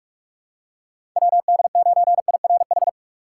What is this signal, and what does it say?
Morse code, heard as a keyed tone, about 46 wpm, 700 Hz. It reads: WD9IRH
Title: USA Call Signs - 6 Characters - 3x Character Spacing - 45wpm